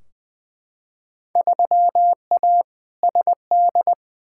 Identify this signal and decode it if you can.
Morse code, heard as a keyed tone, about 21 wpm, 700 Hz. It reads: 3A SD
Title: Field Day Exchanges - 20wpm